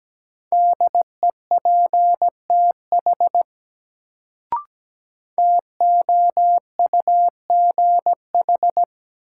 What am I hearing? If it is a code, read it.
Morse code, heard as a keyed tone, about 17 wpm, 700 Hz. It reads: DEPTH TOUGH
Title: Sentences from Top 1200 Words - New Word Drill - 17wpm